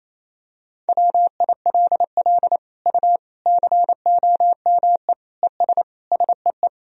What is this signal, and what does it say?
Morse code, heard as a keyed tone, about 28 wpm, 700 Hz. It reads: WILL U COME ES SEE
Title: Sentences from Top 100 Words and Abbreviations 28wpm